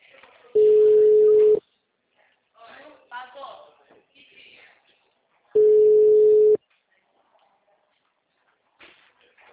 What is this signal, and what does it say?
Ringback tone (Europe, 425 Hz), with speech.